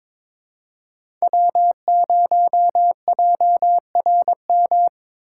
Morse code, heard as a keyed tone, about 22 wpm, 700 Hz. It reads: W0JRM
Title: USA Call Signs - 5 Characters - 4x Character Spacing - 22wpm